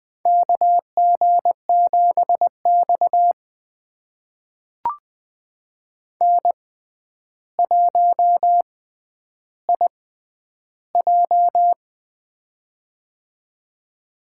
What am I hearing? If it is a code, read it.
Morse code, heard as a keyed tone, about 20 wpm, spaced as for 10 wpm, 700 Hz. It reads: KG7X N1IJ
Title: USA Call Signs - 4 Characters - 6x Character Spacing - 20wpm